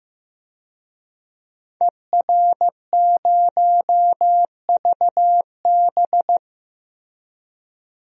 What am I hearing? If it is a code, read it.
Morse code, heard as a keyed tone, about 15 wpm, 700 Hz. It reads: ER0VB
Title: DXCC-100 Call Signs - 5 Characters - Repeated 3x - 15wpm